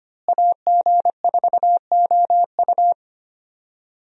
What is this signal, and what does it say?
Morse code, heard as a keyed tone, about 25 wpm, 700 Hz. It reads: AG4OU
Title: USA Call Signs - 5 Characters - Repeated 3x - 25wpm